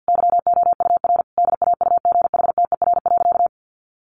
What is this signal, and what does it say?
Morse code, heard as a keyed tone, about 50 wpm, 700 Hz. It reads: YOUR DAUGHTER.
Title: Sentences from Top 1500 Words 50wpm